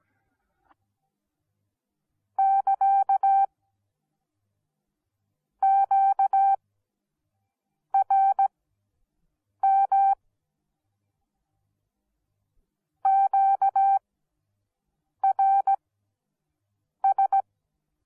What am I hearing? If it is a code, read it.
Morse code, heard as a keyed tone, about 17 wpm, spaced as for 6 wpm, 780 Hz. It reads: <KA> QRM QRS